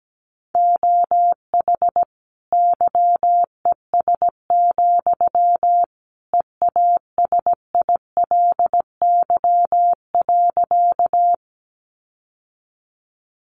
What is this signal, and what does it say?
Morse code, heard as a keyed tone, about 17 wpm, 700 Hz. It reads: OH YES, EASILY.
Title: Sentences from Top 900 Words 17wpm